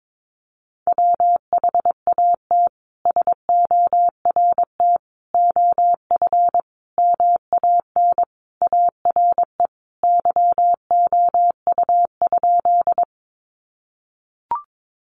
Morse code, heard as a keyed tone, about 22 wpm, 700 Hz. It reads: WHAT SORT OF MAN ARE YOU?
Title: Sentences from Top 1300 Words 22wpm